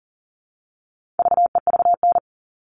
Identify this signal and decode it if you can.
Morse code, heard as a keyed tone, about 40 wpm, 700 Hz. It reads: VEVN
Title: All Cut Numbers - 4-Digit 40wpm